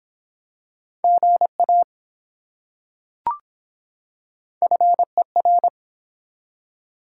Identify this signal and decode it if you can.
Morse code, heard as a keyed tone, about 24 wpm, 700 Hz. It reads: GA FER